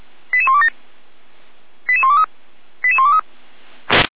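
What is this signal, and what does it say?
Five-tone selective call (ZVEI), heard as quick runs of five tones, tones of about 70 ms. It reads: 88127 88123 88123
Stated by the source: ZVEI five-tone call received by SDR, fire brigade Feuerwehr Orlamünde.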